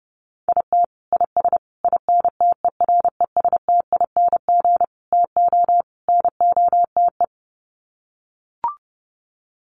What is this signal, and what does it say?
Morse code, heard as a keyed tone, about 30 wpm, 700 Hz. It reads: IT IS INTERESTING TO NOTE
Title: Sets of 5 Words Encore2 30wpm